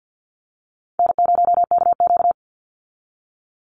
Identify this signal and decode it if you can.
Morse code, heard as a keyed tone, about 50 wpm, 700 Hz. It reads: N0KQ
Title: Easy Call Signs Repeated 2x - 50wpm